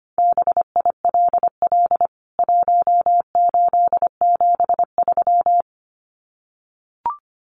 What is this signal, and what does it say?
Morse code, heard as a keyed tone, about 25 wpm, 690 Hz. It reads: BILL 1873
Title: CWT Exchange 25wpm